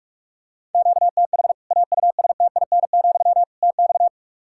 Morse code, heard as a keyed tone, about 46 wpm, 700 Hz. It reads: QTH AUSTIN, TX
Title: Practice QSO - Exchange 1 - Beginning - 45wpm